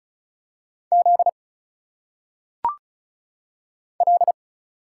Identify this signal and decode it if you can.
Morse code, heard as a keyed tone, about 35 wpm, 700 Hz. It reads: Z L
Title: Single Letters - Phonetic Pronunciation 35wpm